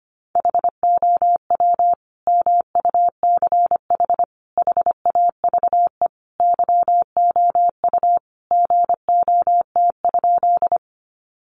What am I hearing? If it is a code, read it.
Morse code, heard as a keyed tone, about 25 wpm, 700 Hz. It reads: HOW MUCH HAVE YOU GOT?
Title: Sentences from Top 300 Words 25wpm